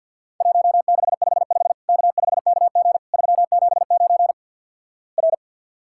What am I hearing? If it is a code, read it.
Morse code, heard as a keyed tone, about 50 wpm, 700 Hz. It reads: 1655 K5KG 379 R
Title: Worked All Europe DX Contest - QTC Exchange with Repeats - 50wpm